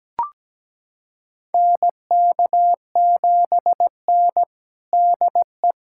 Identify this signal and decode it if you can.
Morse code, heard as a keyed tone, about 17 wpm, 700 Hz. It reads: NK7N DE
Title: USA Callsign DE Callsign 17wpm